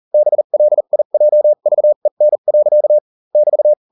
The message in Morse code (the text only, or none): BLIJVEN. <BT>